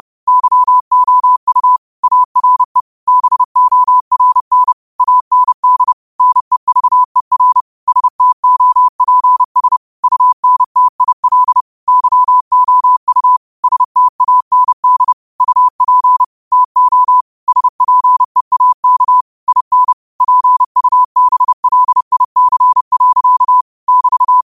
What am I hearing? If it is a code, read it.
Morse code, heard as a keyed tone, about 30 wpm, 1000 Hz. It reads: YOU ARE BORN AND NEVER STOPS UNTIL YOU STAND UP TO SPEAK IN PUBLIC. <BT>